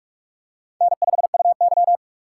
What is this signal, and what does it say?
Morse code, heard as a keyed tone, about 44 wpm, 700 Hz. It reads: N5UY
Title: USA Call Signs - 4 Characters - 3x Character Spacing - 45wpm